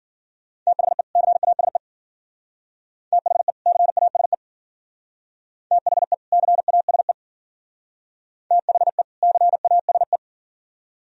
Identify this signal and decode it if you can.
Morse code, heard as a keyed tone, about 51 wpm, 700 Hz. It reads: THE CASE THE CASE THE CASE THE CASE